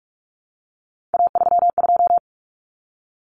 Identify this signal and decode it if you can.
Morse code, heard as a keyed tone, about 44 wpm, 700 Hz. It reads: A32
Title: Common Cut Numbers - 3-Digit 45wpm